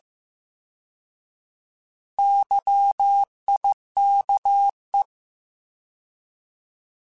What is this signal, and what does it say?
Morse code, heard as a keyed tone, about 15 wpm, 800 Hz. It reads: YIKE